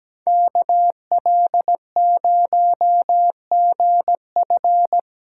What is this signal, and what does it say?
Morse code, heard as a keyed tone, about 20 wpm, 700 Hz. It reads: KL0GF
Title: Speed Racing - USA 2x2 Call Signs 17wpm